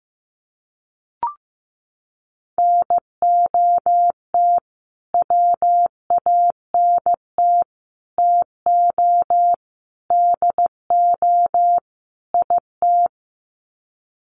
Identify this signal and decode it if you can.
Morse code, heard as a keyed tone, about 15 wpm, 700 Hz. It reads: NOT WANT TO DO IT
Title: Sets of 5 Words Encore 15wpm